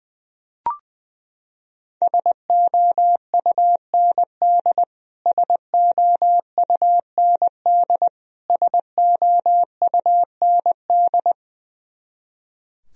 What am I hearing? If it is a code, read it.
Morse code, heard as a keyed tone, about 20 wpm, 700 Hz. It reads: SOUND SOUND SOUND
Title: Top 100 Words Repeated 20wpm